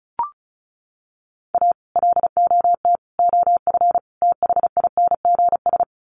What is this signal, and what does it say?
Morse code, heard as a keyed tone, about 34 wpm, 700 Hz. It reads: A LOT OF THINGS
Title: Sets of 4 Words 35wpm